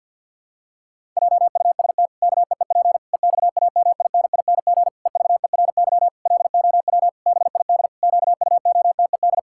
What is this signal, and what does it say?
Morse code, heard as a keyed tone, about 50 wpm, 700 Hz. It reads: JUST KEEP EXAMINING EVERY LOW BID QUOTED
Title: Pangram Challenge v2 - 50wpm